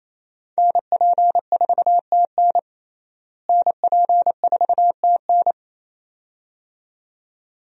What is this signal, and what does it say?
Morse code, heard as a keyed tone, about 28 wpm, 700 Hz. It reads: NP4TN NP4TN
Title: USA Call Signs - 5 Characters - Repeated 2x - 28wpm